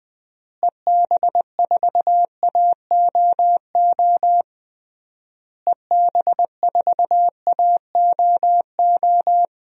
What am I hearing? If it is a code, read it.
Morse code, heard as a keyed tone, about 20 wpm, 700 Hz. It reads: EB4AOO EB4AOO
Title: DXCC-100 Call Signs - 6 Characters - Repeated 3x - 20wpm